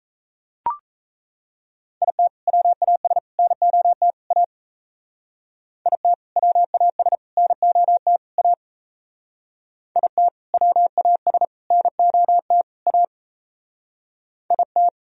Morse code, heard as a keyed tone, about 34 wpm, 700 Hz. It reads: IT WAS NOT A IT WAS NOT A IT WAS NOT A IT
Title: Speed-Racing - Sets of 4 Words 28wpm